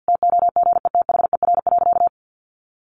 Morse code, heard as a keyed tone, about 50 wpm, 700 Hz. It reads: TOGETHER.
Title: Easy Phrases 50wpm